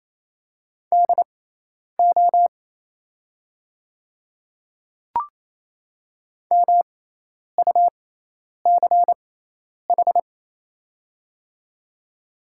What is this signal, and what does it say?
Morse code, heard as a keyed tone, about 28 wpm, spaced as for 10 wpm, 700 Hz. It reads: DO MUCH
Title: 6x Character Spacing - Top 200 Words - 28wpm